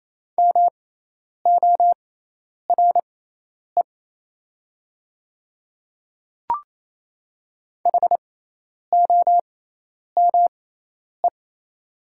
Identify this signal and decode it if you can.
Morse code, heard as a keyed tone, about 28 wpm, spaced as for 10 wpm, 700 Hz. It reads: MORE HOME
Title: 6x Character Spacing - Top 300 Words - 28wpm